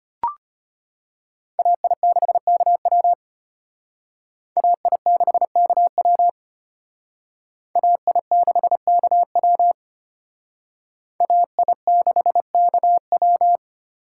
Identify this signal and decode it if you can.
Morse code, heard as a keyed tone, about 33 wpm, 700 Hz. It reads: AI6KW AI6KW AI6KW AI6KW